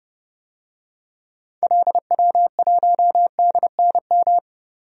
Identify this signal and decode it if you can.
Morse code, heard as a keyed tone, about 30 wpm, 700 Hz. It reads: LW1DNM
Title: DXCC-100 Call Signs - 6 Characters - 30wpm